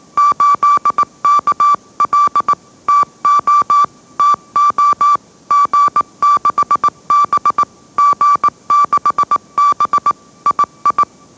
Morse code, heard as a keyed tone, about 18 wpm, 1160 Hz. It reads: 8KL TO TO G6B G6B II
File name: mrx / mrx082 20wpm